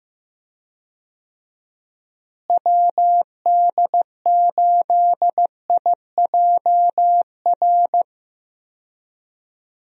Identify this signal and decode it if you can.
Morse code, heard as a keyed tone, about 15 wpm, 700 Hz. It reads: WD8IJR
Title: USA Call Signs - 6 Characters - Repeated 3x - 15wpm